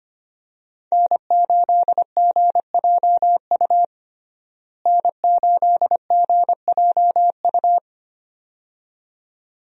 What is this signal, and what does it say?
Morse code, heard as a keyed tone, about 25 wpm, 700 Hz. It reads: N8GJU N8GJU
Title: USA Call Signs - 5 Characters - Repeated 2x - 25wpm